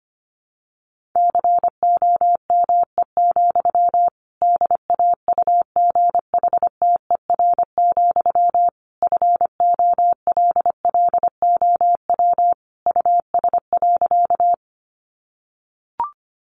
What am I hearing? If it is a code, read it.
Morse code, heard as a keyed tone, about 25 wpm, 700 Hz. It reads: COME, DAUGHTER, FOLLOW US.
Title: Sentences from Top 1500 Words 25wpm